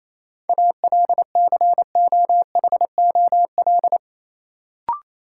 Morse code, heard as a keyed tone, about 28 wpm, 700 Hz. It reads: ALCOHOL
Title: Sentences from Top 1400 Words - New Word Drill - 28wpm